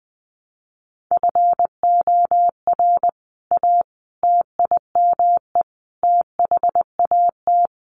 Morse code, heard as a keyed tone, about 20 wpm, 700 Hz. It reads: FOR A TIME THAT
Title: Sentences from Top 700 Words 20wpm